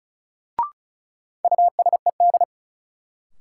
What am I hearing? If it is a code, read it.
Morse code, heard as a keyed tone, about 36 wpm, 700 Hz. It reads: USED
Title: Sentences from Top 600 Words - New Word Drill - 35wpm